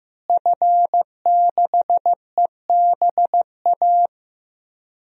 Morse code, heard as a keyed tone, about 15 wpm, 700 Hz. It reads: F6EBA
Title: DXCC-100 Call Signs - 5 Characters - Repeated 3x - 15wpm